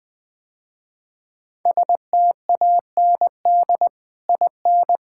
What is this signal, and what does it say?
Morse code, heard as a keyed tone, about 21 wpm, 700 Hz. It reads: STAND IN
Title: Sentences with 3-Word Phrases - Spoken Beginning - 20wpm